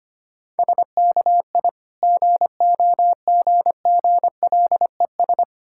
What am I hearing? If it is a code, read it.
Morse code, heard as a keyed tone, about 25 wpm, 700 Hz. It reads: SKI GOGGLES
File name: practice / 25wpm / Difficult Words with Hints - 25wpm